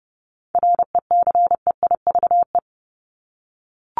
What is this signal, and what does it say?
Morse code, heard as a keyed tone, about 30 wpm, 700 Hz. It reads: RECEIVE
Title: Top 751-1000 Words 30wpm